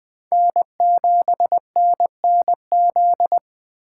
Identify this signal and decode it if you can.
Morse code, heard as a keyed tone, about 20 wpm, 700 Hz. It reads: N7NNZ
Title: USA Call Signs - 3x Character Spacing - 20wpm